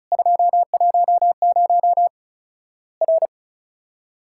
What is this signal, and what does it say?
Morse code, heard as a keyed tone, about 36 wpm, 700 Hz. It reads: 210 R
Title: Worked All Europe DX Contest - QTC Exchange - 35wpm